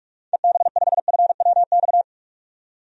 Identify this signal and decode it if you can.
Morse code, heard as a keyed tone, about 45 wpm, 700 Hz. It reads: EB5FWX
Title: DXCC-100 Call Signs - 6 Characters - 45wpm